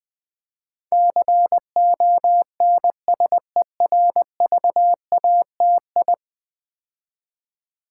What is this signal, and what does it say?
Morse code, heard as a keyed tone, about 20 wpm, 700 Hz. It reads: CONSERVATI
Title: Build A Word - 5 Letters At A Time - 12-Letter Words - 20wpm